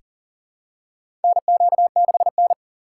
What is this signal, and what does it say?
Morse code, heard as a keyed tone, about 40 wpm, 700 Hz. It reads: NQ6N
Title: USA Call Signs - 4 Characters - 4x Character Spacing - 40wpm